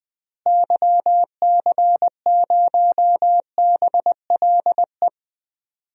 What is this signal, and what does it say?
Morse code, heard as a keyed tone, about 20 wpm, 700 Hz. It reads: YC0BLE
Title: DXCC-100 Call Signs - 6 Characters - Repeated 3x - 20wpm